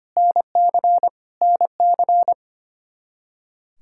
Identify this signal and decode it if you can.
Morse code, heard as a keyed tone, about 25 wpm, 700 Hz. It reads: NC NC
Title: Practice QSO - Exchange 1 - Beginning - 25wpm